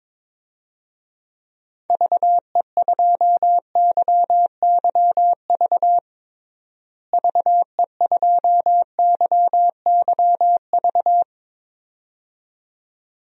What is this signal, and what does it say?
Morse code, heard as a keyed tone, about 22 wpm, 700 Hz. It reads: VE2YYV VE2YYV
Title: DXCC-100 Call Signs - 6 Characters - Repeated 2x - 22wpm